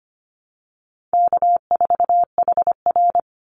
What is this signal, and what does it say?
Morse code, heard as a keyed tone, about 25 wpm, 700 Hz. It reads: K4HR
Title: Easy Call Signs - 25wpm